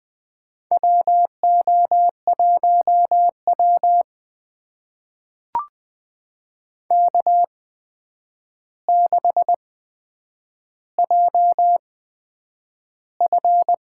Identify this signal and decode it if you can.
Morse code, heard as a keyed tone, about 20 wpm, spaced as for 9 wpm, 700 Hz. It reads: WO1W K6JF
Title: USA Call Signs - 4 Characters - 8x Character Spacing - 20wpm